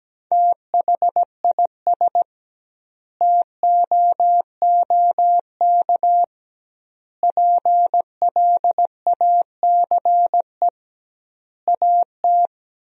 Morse code, heard as a keyed tone, about 17 wpm, 700 Hz. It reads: THIS TOOK PLACE AT
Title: 2x Word-Spacing - Sentences from Top 300 Words - 17wpm